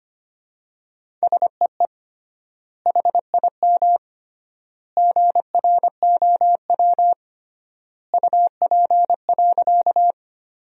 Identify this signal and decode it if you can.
Morse code, heard as a keyed tone, about 25 wpm, 700 Hz. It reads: SEE HIM GROW UP.
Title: 3x Word-Spacing - Sentences from Top 300 Words - 25wpm